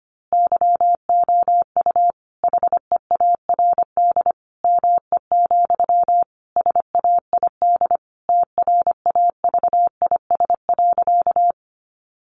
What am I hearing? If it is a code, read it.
Morse code, heard as a keyed tone, about 25 wpm, 700 Hz. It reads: YOU HEARD ME, SAID TRAVIS.
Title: A Sound Of Thunder 25wpm